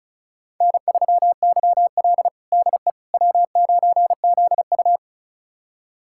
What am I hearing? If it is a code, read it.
Morse code, heard as a keyed tone, about 36 wpm, 700 Hz. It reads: N3YL DE W9ZU
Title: USA Callsign DE Callsign - 4 Characters - 35wpm